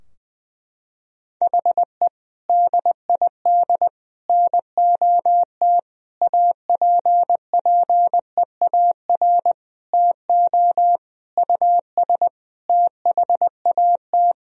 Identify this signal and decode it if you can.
Morse code, heard as a keyed tone, about 21 wpm, 700 Hz. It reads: HE DID NOT APPEAR TO US THAT
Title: Sentences from Top 1000 Words - Review Part 2 - 20wpm